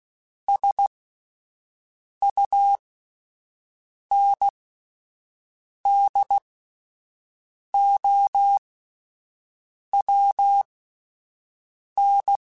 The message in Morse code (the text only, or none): SUNDOWN